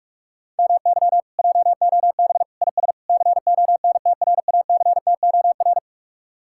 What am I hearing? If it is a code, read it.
Morse code, heard as a keyed tone, about 37 wpm, 700 Hz. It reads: MY JOB IS CONTRACTOR
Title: Speed-Racing Common QSO Phrases - 30wpm